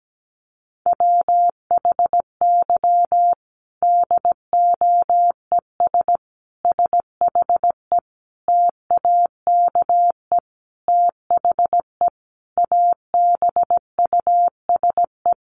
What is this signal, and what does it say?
Morse code, heard as a keyed tone, about 17 wpm, 700 Hz. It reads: WHY DOES SHE TAKE THE ABUSE
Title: Sentences from Top 1300 Words 17wpm